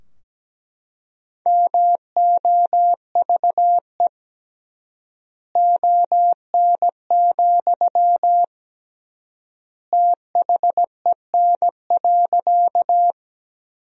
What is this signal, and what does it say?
Morse code, heard as a keyed tone, about 17 wpm, 700 Hz. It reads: MOVE ON, THEN.
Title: 3x Word-Spacing - Sentences from Top 200 Words - 17wpm